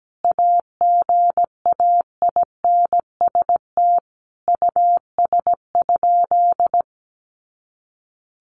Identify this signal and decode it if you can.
Morse code, heard as a keyed tone, about 17 wpm, 700 Hz. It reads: AGAINST US?
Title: Sentences from Top 1500 Words 17wpm